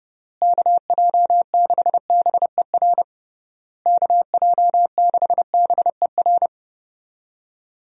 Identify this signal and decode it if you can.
Morse code, heard as a keyed tone, about 30 wpm, 700 Hz. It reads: KJ6BER KJ6BER
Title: USA Call Signs - 6 Characters - Repeated 3x - 30wpm